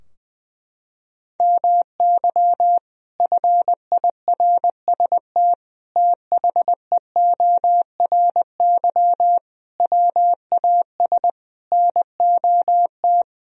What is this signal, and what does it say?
Morse code, heard as a keyed tone, about 20 wpm, 700 Hz. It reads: MY FIRST THEORY WAS NOT
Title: Sentences from Top 1000 Words - Review Part 3 - 20wpm